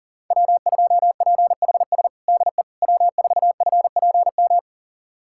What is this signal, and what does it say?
Morse code, heard as a keyed tone, about 37 wpm, 700 Hz. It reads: W2PHS DE W4FPM
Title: USA Callsign DE Callsign - 5 Characters - 40wpm